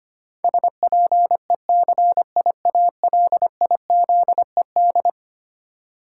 Morse code, heard as a keyed tone, about 25 wpm, 700 Hz. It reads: SPECIALIZED
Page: morse-code-ninja-practice-25wpm